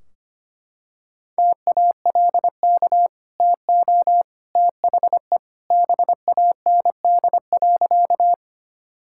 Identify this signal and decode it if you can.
Morse code, heard as a keyed tone, about 25 wpm, 700 Hz. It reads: TALK TO THE BAND.
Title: Sentences from Top 1600 Words 25wpm